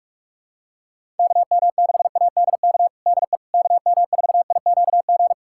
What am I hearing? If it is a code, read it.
Morse code, heard as a keyed tone, about 46 wpm, 700 Hz. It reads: KM6ADK DE KM4IQG
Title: USA Callsign DE Callsign - 6 Characters - 45wpm